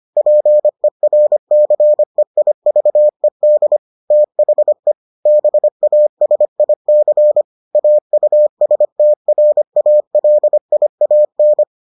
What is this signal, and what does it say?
Morse code, heard as a keyed tone, about 25 wpm, 600 Hz. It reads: PERCEIVED THE BASIC AUSTRALIAN